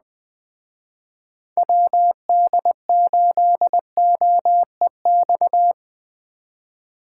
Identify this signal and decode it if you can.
Morse code, heard as a keyed tone, about 20 wpm, 700 Hz. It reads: WD8OEX
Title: Speed Racing - USA Call Signs 17wpm